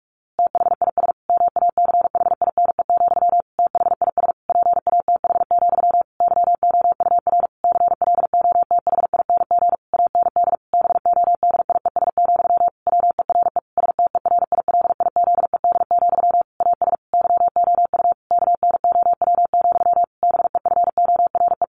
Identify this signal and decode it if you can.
Morse code, heard as a keyed tone, about 44 wpm, 700 Hz. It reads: THIS MACHINE, THIS PATH, YOUR CLOTHING AND BODIES, WERE STERILIZED, AS YOU KNOW, BEFORE
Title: A Sound Of Thunder 45wpm